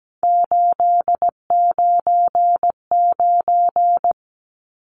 Morse code, heard as a keyed tone, about 17 wpm, 700 Hz. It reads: 899
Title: Numbers - 3-Digit 17wpm